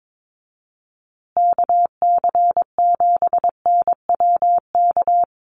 Morse code, heard as a keyed tone, about 22 wpm, 700 Hz. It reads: KC7NWK
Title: USA Call Signs Repeated 3x 22wpm